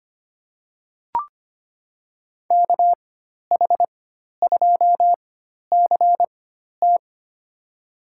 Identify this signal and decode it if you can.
Morse code, heard as a keyed tone, about 25 wpm, spaced as for 12 wpm, 700 Hz. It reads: KH2CT
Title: USA Call Signs - 5 Characters - 4x Character Spacing - 25wpm